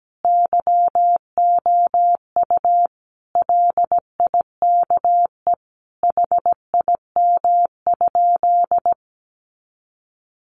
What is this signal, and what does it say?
Morse code, heard as a keyed tone, about 17 wpm, 700 Hz. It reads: YOU LIKE HIM?
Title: Sentences from Top 100 Words 17wpm